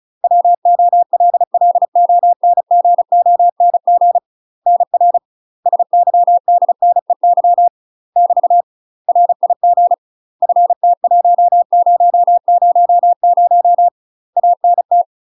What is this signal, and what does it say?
Morse code, heard as a keyed tone, about 36 wpm, 700 Hz. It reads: WOLLONGONG NR SYDNEY <BT> RIG FT1000 ANT
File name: UK MASTER QSOs TRAINING / 35WPM / UK MASTER QSOs 0035